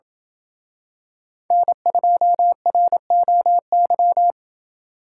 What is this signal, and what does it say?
Morse code, heard as a keyed tone, about 26 wpm, 700 Hz. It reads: N2ROY